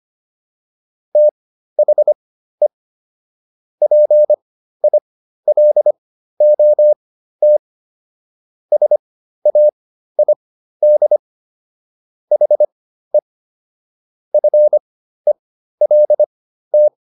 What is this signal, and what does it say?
Morse code, heard as a keyed tone, about 25 wpm, spaced as for 13 wpm, 600 Hz. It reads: THE PILOT SAID HE FELT